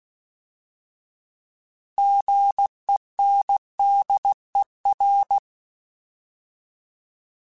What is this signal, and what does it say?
Morse code, heard as a keyed tone, about 16 wpm, 800 Hz. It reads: GENDER
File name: Alphabetical / Alphabetical mp3s / Gs @15